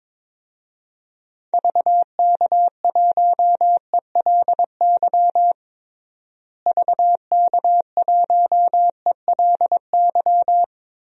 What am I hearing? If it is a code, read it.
Morse code, heard as a keyed tone, about 22 wpm, 700 Hz. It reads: VK1ELY VK1ELY
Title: DXCC-100 Call Signs - 6 Characters - Repeated 3x - 22wpm